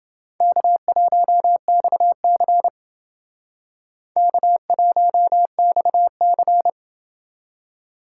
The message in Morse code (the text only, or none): K1XC K1XC